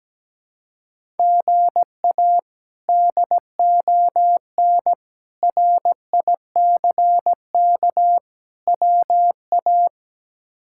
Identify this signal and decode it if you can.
Morse code, heard as a keyed tone, about 17 wpm, 700 Hz. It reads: GA DON RICK WA
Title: K1USN SST QSO Practice - 17wpm